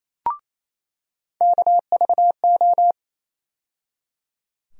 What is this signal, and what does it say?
Morse code, heard as a keyed tone, about 27 wpm, 700 Hz. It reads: KVO
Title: USA Call Sign Postfixes - 3 Letters 28wpm